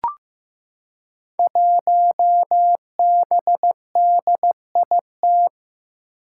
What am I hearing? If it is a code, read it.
Morse code, heard as a keyed tone, about 15 wpm, 700 Hz. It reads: 1BDIT
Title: Five Challenging Letter-Number Combinations 15wpm